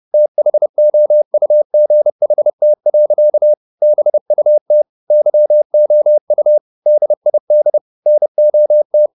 Morse code, heard as a keyed tone, about 30 wpm, 600 Hz. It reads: THOUGHT. BUT YOU DID NOT